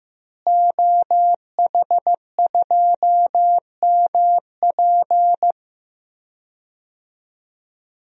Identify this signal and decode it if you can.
Morse code, heard as a keyed tone, about 15 wpm, 700 Hz. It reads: OH2MP
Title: DXCC-100 Call Signs - 5 Characters - Repeated 2x - 15wpm